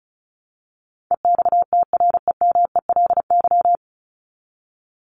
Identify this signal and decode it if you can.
Morse code, heard as a keyed tone, about 34 wpm, 700 Hz. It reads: EXTREMELY
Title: Top 751-1000 Words 35wpm